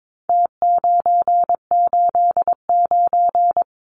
Morse code, heard as a keyed tone, about 22 wpm, 700 Hz. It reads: T989
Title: Numbers - Contest Style - 4-Digit 22wpm